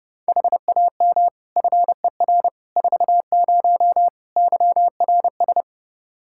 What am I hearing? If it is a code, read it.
Morse code, heard as a keyed tone, about 30 wpm, 700 Hz. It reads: HAM FER 40 YRS
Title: Practice QSO - Exchange 3 - More Info - 30wpm